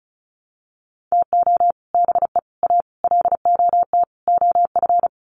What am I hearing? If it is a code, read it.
Morse code, heard as a keyed tone, about 34 wpm, 700 Hz. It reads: TO BE A LOT OF
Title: Sets of 5 Words Encore 35wpm